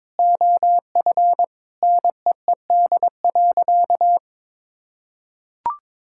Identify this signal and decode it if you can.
Morse code, heard as a keyed tone, about 22 wpm, 700 Hz. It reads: OF NEED.
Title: Sentences from Top 1000 Words - Review Part 5 - 22wpm